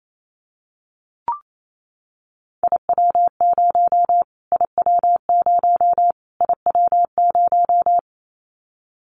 Morse code, heard as a keyed tone, about 28 wpm, 700 Hz. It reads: IW0 IW0 IW0
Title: Callsign Prefixes for Top 25 DXCC - Repeated 3x - 28wpm